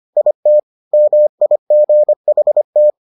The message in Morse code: IT MIGHT

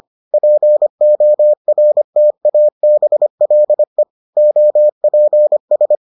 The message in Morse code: PORTABLE OPS

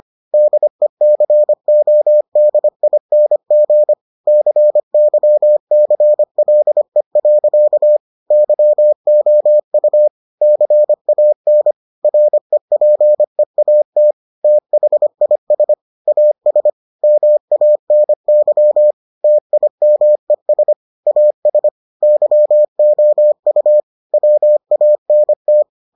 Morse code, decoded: DECODING CYCLE. YOU CAN REPEAT THIS AS MANY TIMES AS YOU WANT

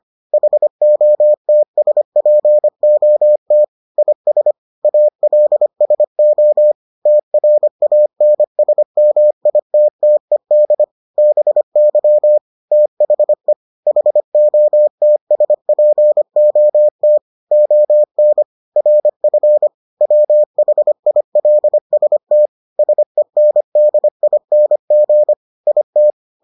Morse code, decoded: HOTSPOT IS ALSO TRANSMITTED BY THE HOTSPOT ON RF WHILST SENDING IT

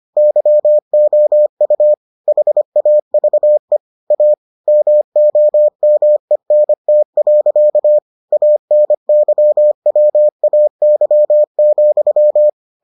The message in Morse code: YOU HAVE A MOMENT. ANYWAY,

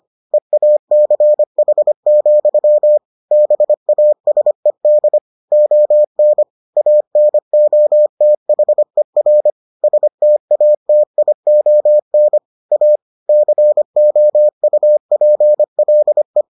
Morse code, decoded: EACH, BASED ON ANOTHER STATION A COUPLE

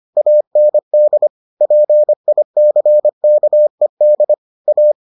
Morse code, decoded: AND PICKED A

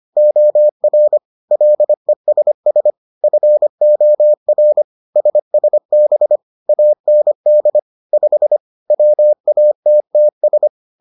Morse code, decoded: OR LESS FOR SSB AND 5 WATTS

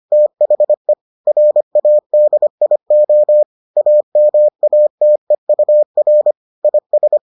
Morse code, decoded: THE RADIO AMATEUR IS

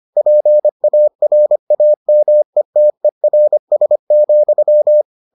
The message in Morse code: PARAMETERS,

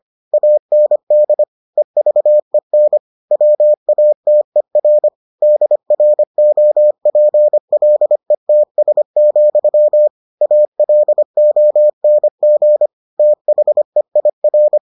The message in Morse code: AND EVEN WATER DROPLETS, ALONG THEIR

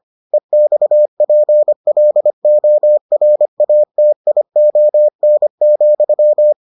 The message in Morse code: EXPLORATION,

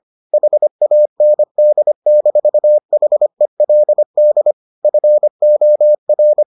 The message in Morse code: HAND-HELD FOR

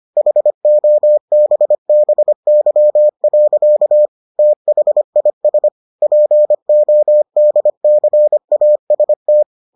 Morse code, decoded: HOBBY. THIS PODCAST

